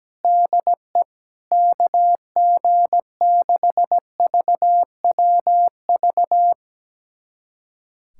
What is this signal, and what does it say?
Morse code, heard as a keyed tone, about 17 wpm, 700 Hz. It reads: DE KG6VWV
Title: USA Callsign DE Callsign - 6 Characters - 17wpm